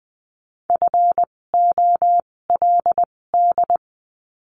Morse code, heard as a keyed tone, about 20 wpm, spaced as for 16 wpm, 700 Hz. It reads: FOLD
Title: 256 Lesson 42a - Words 12-20wpm